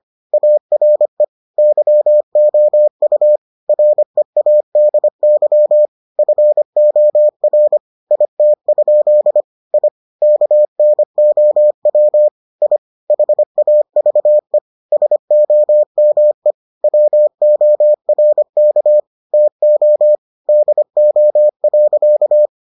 Morse code, heard as a keyed tone, about 25 wpm, 600 Hz. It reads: ARE YOU READY FOR IT? I KNOW I HAVE SOME WORK TO DO.